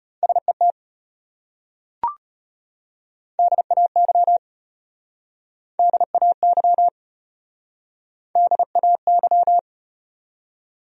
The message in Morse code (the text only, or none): SET DAY DAY DAY